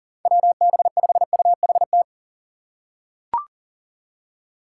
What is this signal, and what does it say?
Morse code, heard as a keyed tone, about 40 wpm, 700 Hz. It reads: WB5UHT
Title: USA Call Signs - 6x Character Spacing - 40wpm